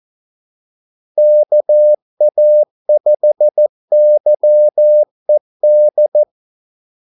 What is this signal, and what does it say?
Morse code, heard as a keyed tone, about 14 wpm, 600 Hz. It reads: KA5YED
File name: Callsigns-14wpm